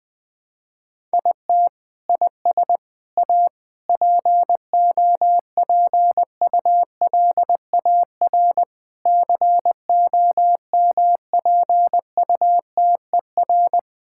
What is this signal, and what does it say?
Morse code, heard as a keyed tone, about 20 wpm, 700 Hz. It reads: IT IS A POPULAR COMPUTER
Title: Sentences from Top 1000 Words - Review Part 6 - 20wpm